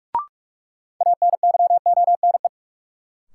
Morse code, heard as a keyed tone, about 46 wpm, 700 Hz. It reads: ANYONE
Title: Sets of 5 Words Encore - New Word Drill - 45wpm